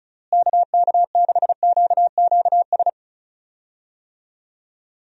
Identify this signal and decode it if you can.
Morse code, heard as a keyed tone, about 34 wpm, 700 Hz. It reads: KK6QQS
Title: USA Call Signs - 6 Characters - Repeated 2x - 35wpm